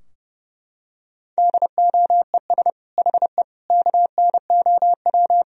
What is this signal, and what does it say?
Morse code, heard as a keyed tone, about 30 wpm, 700 Hz. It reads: DOES HE KNOW
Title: Sentences from Top 1000 Words - Review Part 4 - 30wpm